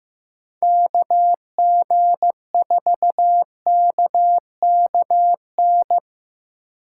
Morse code, heard as a keyed tone, about 15 wpm, 700 Hz. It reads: KG4KKN
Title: USA Call Signs - 6 Characters - 2x Character Spacing - 15wpm